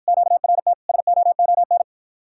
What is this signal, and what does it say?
Morse code, heard as a keyed tone, about 53 wpm, 700 Hz. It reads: QRT SOON